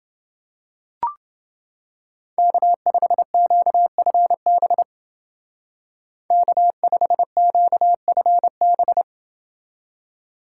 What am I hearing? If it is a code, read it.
Morse code, heard as a keyed tone, about 29 wpm, 700 Hz. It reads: K5QFB K5QFB